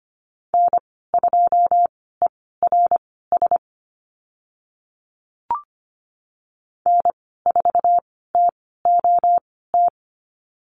Morse code, heard as a keyed tone, about 25 wpm, spaced as for 16 wpm, 690 Hz. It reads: N2ERS N4TOT
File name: Lesson 15g - Number 5 - Call Signs 25wpm